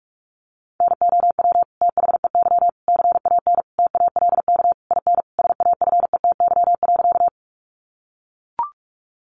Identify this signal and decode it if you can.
Morse code, heard as a keyed tone, about 44 wpm, 700 Hz. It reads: NOW THEY CAN TALK IN SAFETY.
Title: Sentences from Top 700 Words 45wpm